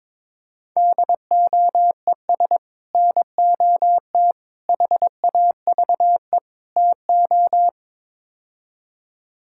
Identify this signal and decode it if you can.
Morse code, heard as a keyed tone, about 23 wpm, 700 Hz. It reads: DOES NOT HAVE TO
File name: practice / 22wpm / Build A Phrase - 5-Word Phrases - 22wpm